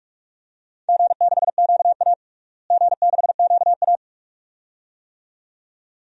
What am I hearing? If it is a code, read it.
Morse code, heard as a keyed tone, about 42 wpm, 700 Hz. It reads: G6QA G6QA
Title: DXCC-100 Call Signs - 4 Characters - Repeated 2x - 45wpm